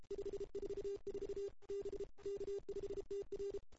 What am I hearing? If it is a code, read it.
Morse code, heard as a keyed tone, about 31 wpm, 410 Hz. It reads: 544 B K5TR
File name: CW / 2008-ss-cw-k5tr-op-n5rz -02:00